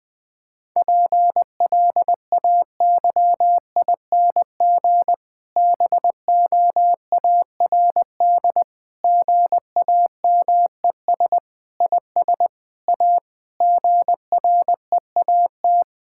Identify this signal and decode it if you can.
Morse code, heard as a keyed tone, about 20 wpm, 700 Hz. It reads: PLAYING BOARD GAMES IS A GREAT
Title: Difficult Sentences with Hints - 20wpm